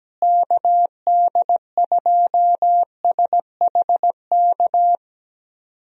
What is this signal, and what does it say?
Morse code, heard as a keyed tone, about 17 wpm, 700 Hz. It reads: KD2SHK
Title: USA Call Signs - 6 Characters - Repeated 3x - 17wpm